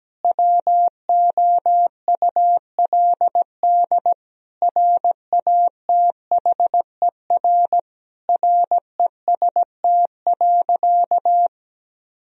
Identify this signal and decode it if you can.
Morse code, heard as a keyed tone, about 17 wpm, 700 Hz. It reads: WOULD RATHER REST.
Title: Sentences from Top 1000 Words - Review Part 4 - 17wpm